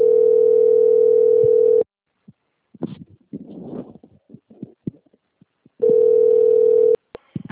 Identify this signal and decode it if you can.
Ringback tone (North America), 440 + 480 Hz.
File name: out- - - - - .8837